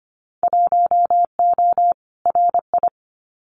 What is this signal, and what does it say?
Morse code, heard as a keyed tone, about 25 wpm, 700 Hz. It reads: 1O RI
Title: Winter Field Day Exchanges 25wpm